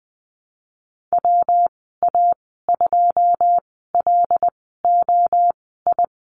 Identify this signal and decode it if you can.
Morse code, heard as a keyed tone, about 20 wpm, spaced as for 15 wpm, 700 Hz. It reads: WA2LOI
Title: 147 Lesson 24g - Letter y - Call Signs 20wpm